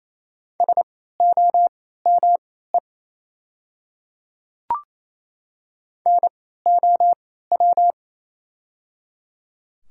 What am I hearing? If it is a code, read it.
Morse code, heard as a keyed tone, about 28 wpm, spaced as for 16 wpm, 700 Hz. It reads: SOME NOW
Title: 3x Character Spacing - Top 100 Words - 28wpm